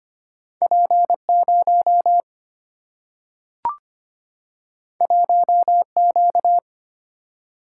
Morse code, heard as a keyed tone, about 25 wpm, 700 Hz. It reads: P 0 1 Q
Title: Single-Letter Single-Number 25wpm